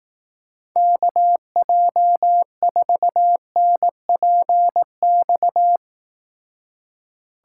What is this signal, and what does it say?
Morse code, heard as a keyed tone, about 18 wpm, 700 Hz. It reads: KJ4NPX
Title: Speed Racing - USA 2x3 Call Signs 15wpm